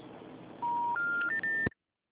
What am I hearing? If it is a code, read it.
Special information tone, with noise.
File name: out- -2030- - - .30209